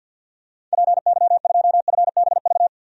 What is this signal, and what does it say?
Morse code, heard as a keyed tone, about 51 wpm, 700 Hz. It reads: PY2FBV
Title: DXCC-100 Call Signs - 6 Characters - 50wpm